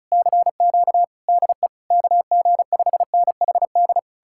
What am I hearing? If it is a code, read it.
Morse code, heard as a keyed tone, about 48 wpm, spaced as for 42 wpm, 700 Hz. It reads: CQ DE KG5NHD
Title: Speed-Racing Common QSO Phrases Encore - 35wpm